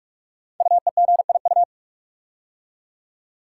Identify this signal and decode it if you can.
Morse code, heard as a keyed tone, about 44 wpm, 700 Hz. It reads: UEGIU
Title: Code Groups - 5 Letters - 45wpm